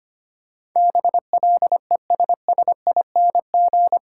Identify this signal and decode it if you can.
Morse code, heard as a keyed tone, about 25 wpm, 700 Hz. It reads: BLESSING
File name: 8 Letter Words - First 3 Letters Given - 25wpm